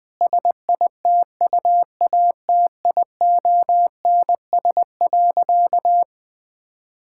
Morse code, heard as a keyed tone, about 20 wpm, 700 Hz. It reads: SITUATIONS.
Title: Difficult Sentences with Hints - 20wpm